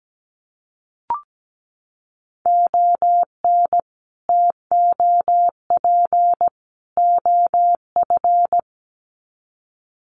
Morse code, heard as a keyed tone, about 17 wpm, 700 Hz. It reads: ON TOP OF
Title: Sets of 3 Words 17wpm